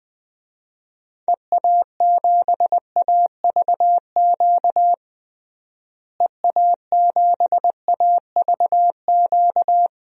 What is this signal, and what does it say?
Morse code, heard as a keyed tone, about 21 wpm, 700 Hz. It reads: EA7AVQ EA7AVQ
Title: DXCC-100 Call Signs - 6 Characters - Repeated 3x - 20wpm